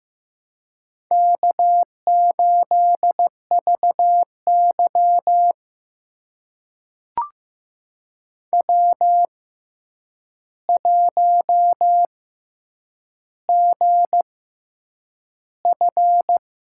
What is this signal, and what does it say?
Morse code, heard as a keyed tone, about 15 wpm, spaced as for 8 wpm, 700 Hz. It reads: K8VY W1GF
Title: USA Call Signs - 4 Characters - 6x Character Spacing - 15wpm